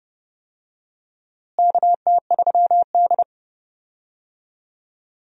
Morse code, heard as a keyed tone, about 31 wpm, 700 Hz. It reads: KT3D
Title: USA Call Signs - 4 Characters - Repeated 2x - 30wpm